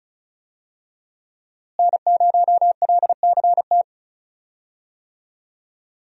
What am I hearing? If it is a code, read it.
Morse code, heard as a keyed tone, about 35 wpm, 700 Hz. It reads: N0LCT
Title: USA Call Signs - 5 Characters - Repeated 2x - 35wpm